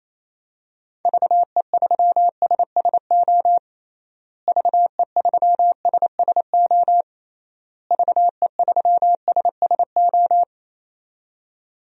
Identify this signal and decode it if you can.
Morse code, heard as a keyed tone, about 28 wpm, 700 Hz. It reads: VE3SSO VE3SSO VE3SSO
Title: DXCC-100 Call Signs - 6 Characters - Repeated 3x - 28wpm